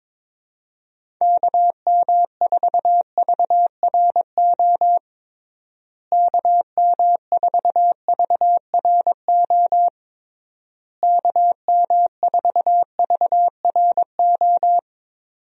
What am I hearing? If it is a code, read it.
Morse code, heard as a keyed tone, about 22 wpm, 700 Hz. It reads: KM4VRO KM4VRO KM4VRO
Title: USA Call Signs - 6 Characters - Repeated 3x - 22wpm